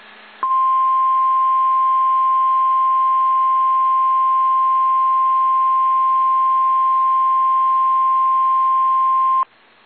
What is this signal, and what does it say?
Emergency Alert System (SAME) tones (the 1050 Hz attention tone). It is heard from 0.4 to 9.5 s.
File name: eas / KSDZKDJL / KSDZKDJL